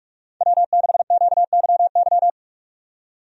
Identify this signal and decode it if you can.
Morse code, heard as a keyed tone, about 46 wpm, 700 Hz. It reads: W6QYY
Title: USA Call Signs - 5 Characters - 45wpm